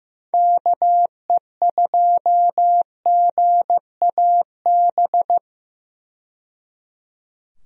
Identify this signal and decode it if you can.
Morse code, heard as a keyed tone, about 15 wpm, 700 Hz. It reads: KE2GAB
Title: USA Call Signs - 6 Characters - 15wpm